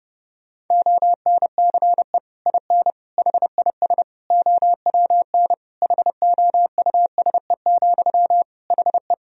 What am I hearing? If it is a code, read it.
Morse code, heard as a keyed tone, about 30 wpm, 700 Hz. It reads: ONCE IN HIS OWN HOUSE, HE
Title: Sentences from Top 1000 Words - Review Part 2 - 30wpm